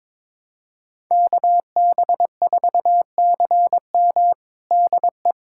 Morse code, heard as a keyed tone, about 22 wpm, 700 Hz. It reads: KB4CM DE
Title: USA Callsign DE Callsign - 5 Characters - 22wpm